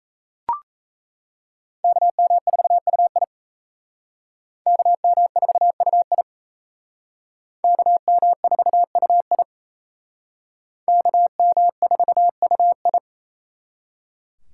Morse code, heard as a keyed tone, about 37 wpm, 700 Hz. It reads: KM4UI KM4UI KM4UI KM4UI